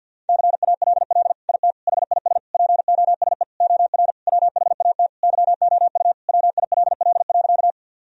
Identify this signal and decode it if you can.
Morse code, heard as a keyed tone, about 48 wpm, 700 Hz. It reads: CALL IT HIS POSE OR WHAT YOU WILL.
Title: Sentences from Top 1000 Words - Review Part 2 - 50wpm